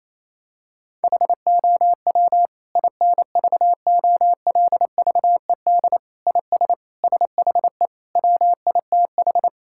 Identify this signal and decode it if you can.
Morse code, heard as a keyed tone, about 28 wpm, 700 Hz. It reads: HOW INVOLVED IS SHE WITH
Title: Sentences from Top 1000 Words - Review Part 5 - 28wpm